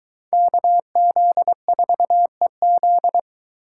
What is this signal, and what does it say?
Morse code, heard as a keyed tone, about 22 wpm, 700 Hz. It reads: KZ4EZ